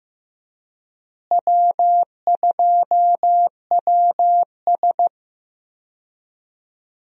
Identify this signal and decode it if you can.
Morse code, heard as a keyed tone, about 16 wpm, 700 Hz. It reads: W2WS
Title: USA Call Signs - 4 Characters - Repeated 3x - 15wpm